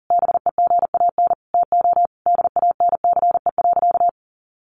Morse code, heard as a keyed tone, about 40 wpm, 700 Hz. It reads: BEGAN TO DANCE.
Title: Sentences from Top 900 Words 40wpm